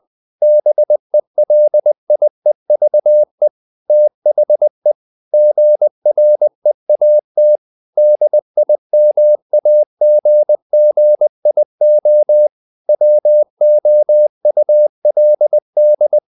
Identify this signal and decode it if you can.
Morse code, heard as a keyed tone, about 20 wpm, 600 Hz. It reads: BELIEVE THE GREAT DIMAGGIO WOULD
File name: the old man and the sea / the old man and the sea part 1 20wpm